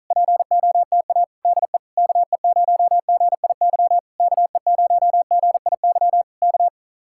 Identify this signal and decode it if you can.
Morse code, heard as a keyed tone, about 46 wpm, 700 Hz. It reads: POTA DE KE0GIY KE0GIY K